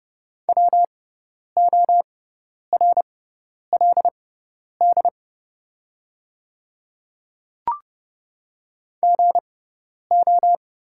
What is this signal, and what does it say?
Morse code, heard as a keyed tone, about 30 wpm, spaced as for 10 wpm, 700 Hz. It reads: WORLD GO
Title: 6x Character Spacing - Top 200 Words - 30wpm